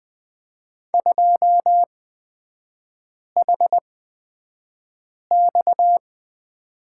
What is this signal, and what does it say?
Morse code, heard as a keyed tone, about 20 wpm, spaced as for 5 wpm, 690 Hz. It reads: 2HX
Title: 229 Lesson 37e - Letter x - Three Characters 20wpm